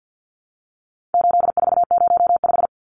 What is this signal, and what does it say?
Morse code, heard as a keyed tone, about 50 wpm, 700 Hz. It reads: 8405
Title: Numbers - 4-Digit 50wpm